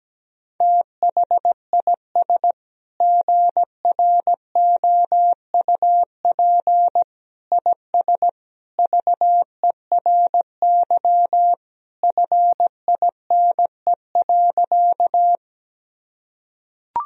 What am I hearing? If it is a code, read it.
Morse code, heard as a keyed tone, about 17 wpm, 700 Hz. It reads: THIS GROUP IS VERY FINE.
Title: Sentences from Top 1000 Words - Review Part 1 - 17wpm